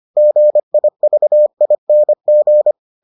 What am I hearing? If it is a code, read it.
Morse code, heard as a keyed tone, about 25 wpm, 600 Hz. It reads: GIVING